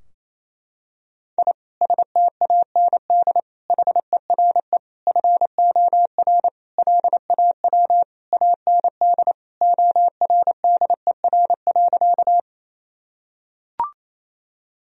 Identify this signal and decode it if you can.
Morse code, heard as a keyed tone, about 28 wpm, 700 Hz. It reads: I STAND HERE FOR LAW AND ORDER.
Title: Sentences from Top 1000 Words - Review Part 3 - 28wpm